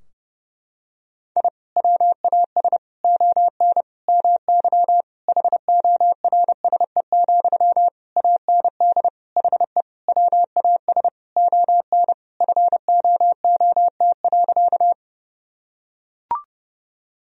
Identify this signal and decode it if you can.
Morse code, heard as a keyed tone, about 30 wpm, 700 Hz. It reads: I WAS ON MY HORSE, AND HE WAS ON FOOT.
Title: Sentences from Top 1000 Words - Review Part 3 - 30wpm